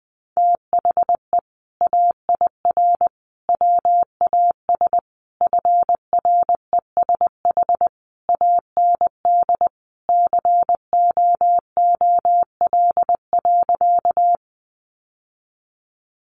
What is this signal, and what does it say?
Morse code, heard as a keyed tone, about 20 wpm, 700 Hz. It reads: THE AIR WAS FRESH AND COOL.
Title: Sentences from Top 1200 Words 20wpm